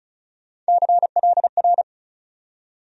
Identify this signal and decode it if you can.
Morse code, heard as a keyed tone, about 34 wpm, 700 Hz. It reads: CLR
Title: USA Call Sign Postfixes - 3 Letters 35wpm